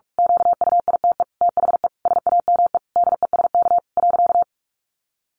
Morse code, heard as a keyed tone, about 44 wpm, 700 Hz. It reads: QUITE THE SAME DESK .
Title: A Sound Of Thunder 45wpm